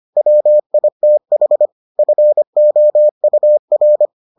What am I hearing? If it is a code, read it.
Morse code, heard as a keyed tone, about 25 wpm, 600 Hz. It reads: WITH FOUR